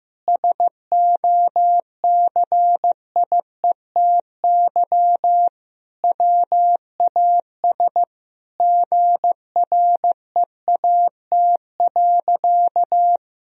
Morse code, heard as a keyed tone, about 15 wpm, 700 Hz. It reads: SOCIETY WAS GREAT.
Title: Sentences from Top 1000 Words - Review Part 4 - 15wpm